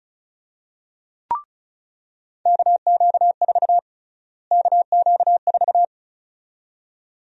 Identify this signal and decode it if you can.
Morse code, heard as a keyed tone, about 34 wpm, 700 Hz. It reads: KQ4 KQ4
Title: USA Call Sign Prefixes - Repeated 2x - 35wpm